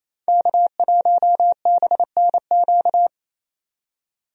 Morse code, heard as a keyed tone, about 28 wpm, 700 Hz. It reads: K1BNQ
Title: USA Call Signs - 5 Characters - 28wpm